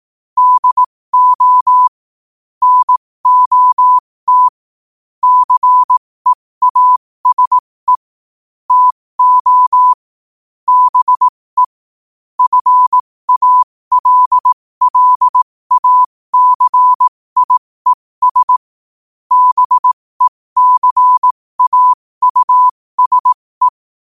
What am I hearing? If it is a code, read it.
Morse code, heard as a keyed tone, about 18 wpm, spaced as for 15 wpm, 1000 Hz. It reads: DO NOT CEASE TO BE FALLACIES BECAUSE
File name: qotdfiles / 15wpm